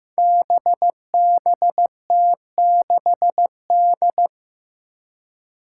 Morse code, heard as a keyed tone, about 15 wpm, 700 Hz. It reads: BBT6D
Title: Five Challenging Letter-Number Combinations 15wpm